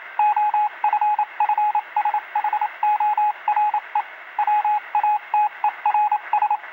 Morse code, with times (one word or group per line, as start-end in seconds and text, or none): 0.2-4.0 OFFSHORE
4.4-6.6 WATERS